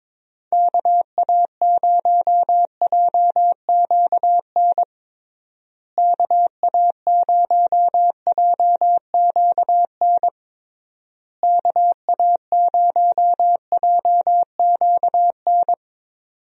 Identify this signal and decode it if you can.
Morse code, heard as a keyed tone, about 22 wpm, 700 Hz. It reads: KA0JQN KA0JQN KA0JQN
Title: USA Call Signs - 6 Characters - Repeated 3x - 22wpm